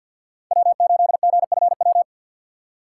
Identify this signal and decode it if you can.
Morse code, heard as a keyed tone, about 48 wpm, 700 Hz. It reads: W8GFW
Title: USA Call Signs - 5 Characters - 6x Character Spacing - 50wpm